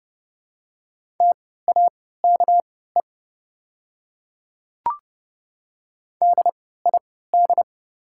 Morse code, heard as a keyed tone, about 29 wpm, spaced as for 17 wpm, 700 Hz. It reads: TAKE DID
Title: 3x Character Spacing - Top 200 Words - 30wpm